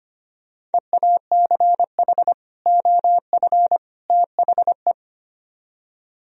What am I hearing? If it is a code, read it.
Morse code, heard as a keyed tone, about 31 wpm, 700 Hz. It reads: EACH OF THE